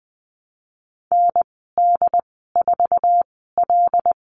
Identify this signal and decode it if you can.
Morse code, heard as a keyed tone, about 20 wpm, spaced as for 14 wpm, 700 Hz. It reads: ND4L
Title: 076 Lesson 13g - Letter L - Call Signs 20wpm